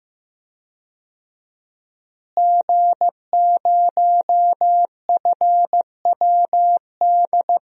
Morse code, heard as a keyed tone, about 15 wpm, 700 Hz. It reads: G0FWD
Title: DXCC-100 Call Signs - 5 Characters - Repeated 2x - 15wpm